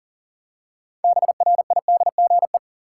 Morse code, heard as a keyed tone, about 40 wpm, 700 Hz. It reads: BRIDGE
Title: Six Letter Words - 40wpm